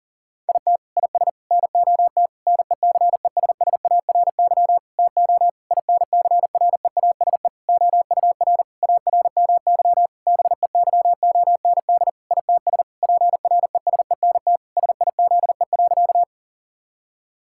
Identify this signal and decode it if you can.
Morse code, heard as a keyed tone, about 40 wpm, 700 Hz. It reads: IT IS NOT NECESSARY TO INCREASE OUR ARMY BEYOND ITS PRESENT SIZE.
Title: Sentences from Top 1000 Words - Review Part 6 - 40wpm